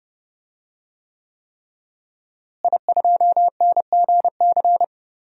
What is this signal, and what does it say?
Morse code, heard as a keyed tone, about 30 wpm, 700 Hz. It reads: I2NGC
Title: DXCC-100 Call Signs - 5 Characters - Repeated 3x - 30wpm